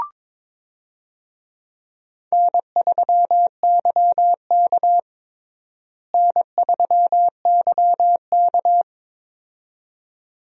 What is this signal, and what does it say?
Morse code, heard as a keyed tone, about 22 wpm, 700 Hz. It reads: N3YK N3YK
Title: USA Call Signs - 4 Characters - Repeated 2x - 22wpm